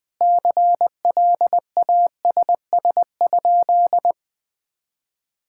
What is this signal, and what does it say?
Morse code, heard as a keyed tone, about 19 wpm, 700 Hz. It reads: CLASS?
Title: Sentences from Top 1000 Words - Review Part 6 - 20wpm